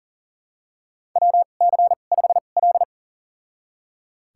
Morse code, heard as a keyed tone, about 38 wpm, spaced as for 28 wpm, 700 Hz. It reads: WC5L
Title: USA Call Signs - 4 Characters - 2x Character Spacing - 40wpm